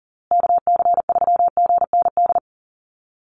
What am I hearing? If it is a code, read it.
Morse code, heard as a keyed tone, about 40 wpm, 700 Hz. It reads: KC3GND
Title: USA Call Signs Repeated 2x 40wpm